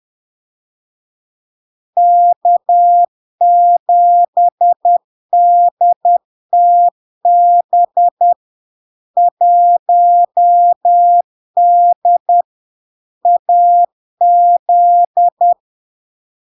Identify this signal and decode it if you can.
Morse code, heard as a keyed tone, about 10 wpm, 700 Hz. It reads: K7DTB 1D AZ
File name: LARK-FD-2022-Log-in-CW-10wpm-700-Hz